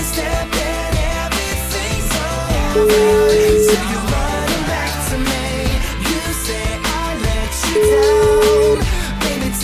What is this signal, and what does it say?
Ringback tone (Europe, 425 Hz), with noise.